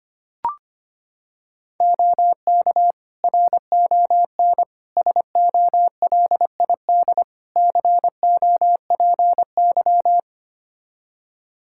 Morse code, heard as a keyed tone, about 25 wpm, 700 Hz. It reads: OK RON SOLID COPY
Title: Practice QSO - Exchange 3 - More Info - 25wpm